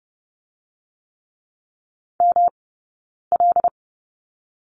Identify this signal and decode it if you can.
Morse code, heard as a keyed tone, about 30 wpm, 700 Hz. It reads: M L
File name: Lesson 18d - Letter m - Two Characters 30wpm